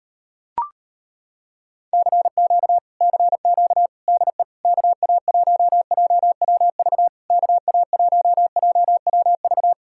Morse code, heard as a keyed tone, about 33 wpm, 700 Hz. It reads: CQ CQ DE KA1JWV KA1JWV